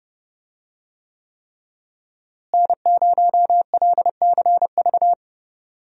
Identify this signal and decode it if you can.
Morse code, heard as a keyed tone, about 30 wpm, 700 Hz. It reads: N0LCV
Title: USA Call Signs - 5 Characters - Repeated 3x - 30wpm